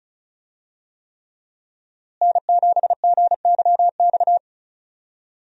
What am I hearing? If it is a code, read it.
Morse code, heard as a keyed tone, about 36 wpm, 700 Hz. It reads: N7GYX
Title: USA Call Signs - 5 Characters - Repeated 3x - 35wpm